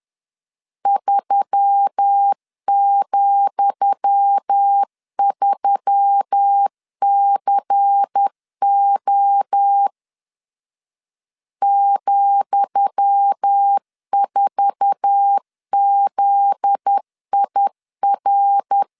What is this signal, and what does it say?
Morse code, heard as a keyed tone, about 10 wpm, 800 Hz. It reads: 3,3CO ,4ZIR
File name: LEZIONI / 10 SIGNS / 10D COMA 10wpm 3013